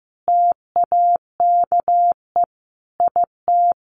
Morse code, heard as a keyed tone, about 15 wpm, 700 Hz. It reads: TAKE IT
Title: A Sound Of Thunder 15wpm